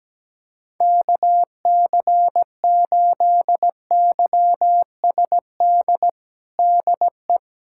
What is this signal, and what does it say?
Morse code, heard as a keyed tone, about 17 wpm, 700 Hz. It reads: KC8YSD DE
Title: USA Callsign DE Callsign - 6 Characters - 17wpm